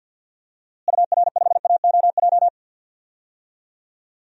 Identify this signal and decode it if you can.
Morse code, heard as a keyed tone, about 49 wpm, 700 Hz. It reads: UR5AOJ
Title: DXCC-100 Call Signs - 6 Characters - Repeated 2x - 50wpm